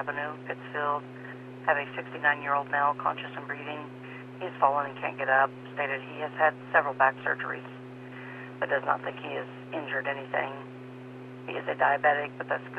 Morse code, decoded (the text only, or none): none